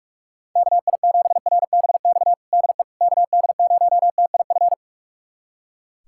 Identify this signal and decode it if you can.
Morse code, heard as a keyed tone, about 41 wpm, 700 Hz. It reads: KI7RBX DE KD0TIF
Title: USA Callsign DE Callsign - 6 Characters - 45wpm